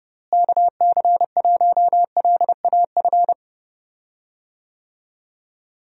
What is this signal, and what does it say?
Morse code, heard as a keyed tone, about 32 wpm, 700 Hz. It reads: KC1LAF
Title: USA Call Signs - 6 Characters - Repeated 3x - 30wpm